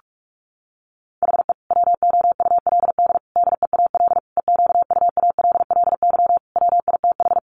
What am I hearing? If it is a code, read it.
Morse code, heard as a keyed tone, about 44 wpm, 700 Hz. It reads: HE WOULD DEAL EQUALLY WITH
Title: Sentences from Top 1400 Words 45wpm